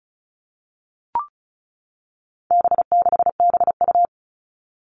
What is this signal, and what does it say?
Morse code, heard as a keyed tone, about 34 wpm, 700 Hz. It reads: B6BU
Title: All Cut Numbers - 4-Digit 35wpm